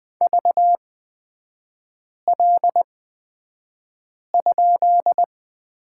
Morse code, heard as a keyed tone, about 20 wpm, spaced as for 5 wpm, 700 Hz. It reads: VL?
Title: 236 Lesson 38e - Letter q - Three Characters 20wpm